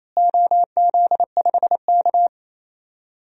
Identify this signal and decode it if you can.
Morse code, heard as a keyed tone, about 28 wpm, 700 Hz. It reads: OZ5K
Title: DXCC-100 Call Signs - 4 Characters - Repeated 2x - 28wpm